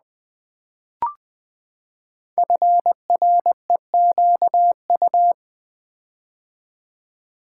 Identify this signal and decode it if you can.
Morse code, heard as a keyed tone, about 20 wpm, 700 Hz. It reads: FREQU
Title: Build A Word - 5 Letters At A Time - 10-Letter Words - 20wpm